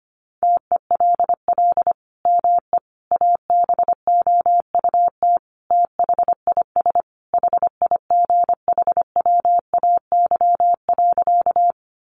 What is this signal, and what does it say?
Morse code, heard as a keyed tone, about 25 wpm, 700 Hz. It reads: TELL ME ABOUT THIS HIGHWAY.
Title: Sentences from Top 1400 Words 25wpm